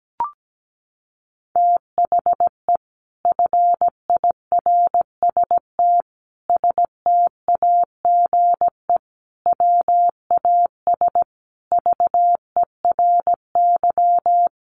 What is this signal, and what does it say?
Morse code, heard as a keyed tone, about 17 wpm, 700 Hz. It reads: THE FIRST STAGE WAS VERY
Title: Sentences from Top 1200 Words 17wpm